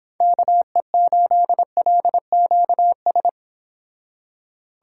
Morse code, heard as a keyed tone, about 30 wpm, 700 Hz. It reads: KE8LQS